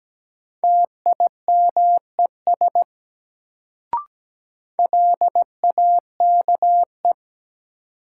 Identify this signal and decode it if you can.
Morse code, heard as a keyed tone, about 17 wpm, 700 Hz. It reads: TIMES LAKE
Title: Sentences from Top 1100 Words - New Word Drill Alt - 17wpm